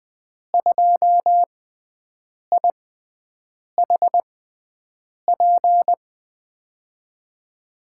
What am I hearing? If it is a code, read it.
Morse code, heard as a keyed tone, about 20 wpm, spaced as for 7 wpm, 700 Hz. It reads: 2IHP
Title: USA Call Signs - 5 Characters - 6x Character Spacing - 20wpm